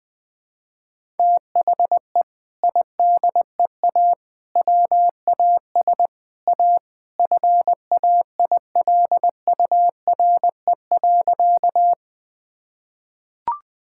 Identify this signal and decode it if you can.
Morse code, heard as a keyed tone, about 21 wpm, 700 Hz. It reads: THE IDEA WAS A FAILURE.
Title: Sentences from Top 1000 Words - Review Part 6 - 20wpm